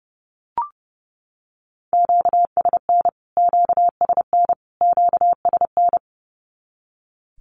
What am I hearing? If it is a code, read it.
Morse code, heard as a keyed tone, about 30 wpm, 700 Hz. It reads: QSN QSN QSN
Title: Q-Codes Repeated 3x 30wpm